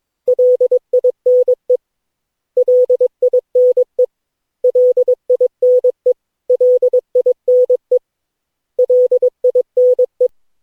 Morse code, heard as a keyed tone, about 22 wpm, 500 Hz. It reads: LINE LINE LINE LINE LINE
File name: cw / Session9 cw training